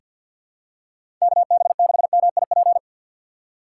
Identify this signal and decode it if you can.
Morse code, heard as a keyed tone, about 50 wpm, 700 Hz. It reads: KB6MIP
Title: USA Call Signs - 4x Character Spacing - 50wpm